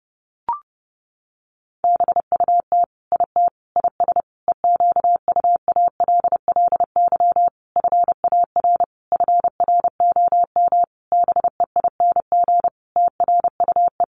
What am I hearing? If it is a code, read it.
Morse code, heard as a keyed tone, about 30 wpm, 700 Hz. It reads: BUT IT IS EQUALLY FAR FROM BEING TRUE
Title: Sentences from Top 1400 Words 30wpm